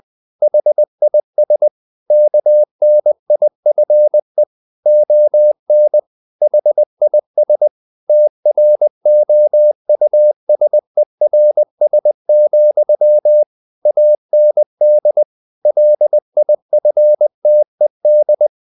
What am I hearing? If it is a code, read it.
Morse code, heard as a keyed tone, about 20 wpm, 600 Hz. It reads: HIS KNIFE ON HIS TROUSERS, AND LIFTED
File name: the old man and the sea / the old man and the sea part 1 20wpm